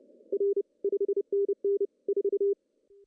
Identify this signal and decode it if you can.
Morse code, heard as a keyed tone, about 29 wpm, 400 Hz. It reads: R 5NN 4
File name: CW / cqww-cw-k5tr